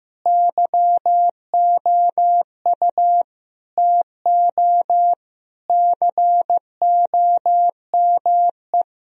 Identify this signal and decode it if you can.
Morse code, heard as a keyed tone, about 15 wpm, 700 Hz. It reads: YOU TO COME
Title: Sentences from Top 1000 Words - Review Part 3 - 15wpm